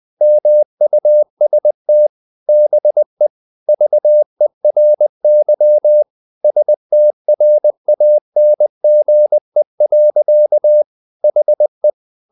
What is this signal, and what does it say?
Morse code, heard as a keyed tone, about 20 wpm, 600 Hz. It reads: MUST BE VERY STRANGE. HE
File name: the old man and the sea / the old man and the sea part 1 20wpm